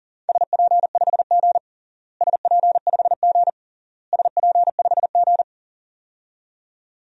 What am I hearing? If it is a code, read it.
Morse code, heard as a keyed tone, about 40 wpm, 700 Hz. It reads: SP5G SP5G SP5G
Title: DXCC-100 Call Signs - 4 Characters - Repeated 3x - 40wpm